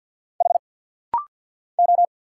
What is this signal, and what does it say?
Morse code, heard as a keyed tone, about 50 wpm, 700 Hz. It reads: H O
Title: Single Letter-Number Alt - Rapid-Fire 50wpm